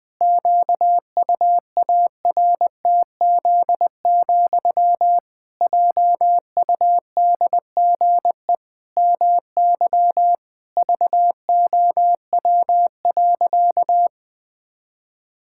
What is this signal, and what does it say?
Morse code, heard as a keyed tone, about 20 wpm, 700 Hz. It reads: QUARTZ, JUDGE MY VOW.
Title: Pangram Challenge v2 - 20wpm